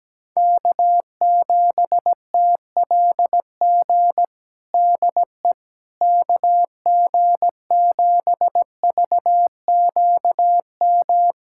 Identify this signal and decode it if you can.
Morse code, heard as a keyed tone, about 17 wpm, 700 Hz. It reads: K7TLG DE KG7VQM
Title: USA Callsign DE Callsign 17wpm